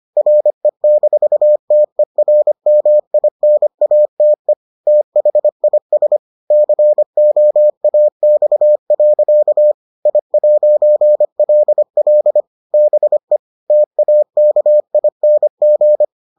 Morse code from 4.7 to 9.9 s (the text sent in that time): THIS COAX.